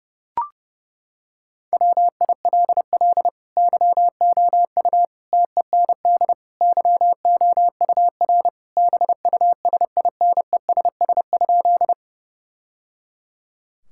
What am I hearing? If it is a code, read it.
Morse code, heard as a keyed tone, about 32 wpm, 700 Hz. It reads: WILL YOU TEND YOUR BUSINESS?
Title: Sentences from Top 1000 Words - Review Part 5 - 30wpm